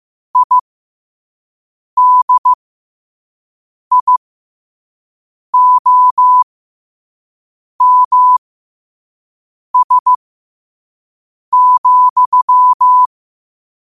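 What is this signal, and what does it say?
Morse code, heard as a keyed tone, about 15 wpm, spaced as for 5 wpm, 1000 Hz. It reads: IDIOMS,
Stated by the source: Morse code sent at 5 wpm.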